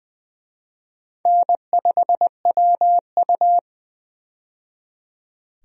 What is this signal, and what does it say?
Morse code, heard as a keyed tone, about 20 wpm, 700 Hz. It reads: N5WU
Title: USA Call Signs - 4 Characters - 20wpm